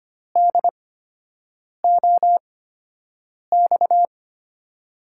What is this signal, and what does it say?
Morse code, heard as a keyed tone, about 25 wpm, spaced as for 7 wpm, 700 Hz. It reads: DOX
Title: USA Call Signs - 6 Characters - 8x Character Spacing - 25wpm